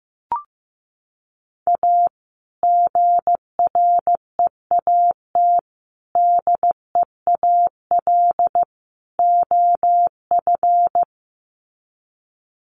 Morse code, heard as a keyed tone, about 15 wpm, 700 Hz. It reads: A GREAT DEAL OF
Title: Sets of 4 Words 15wpm